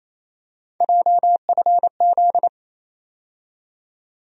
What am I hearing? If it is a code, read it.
Morse code, heard as a keyed tone, about 28 wpm, 700 Hz. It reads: JFZ
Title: USA Call Sign Postfixes - 3 Letters 28wpm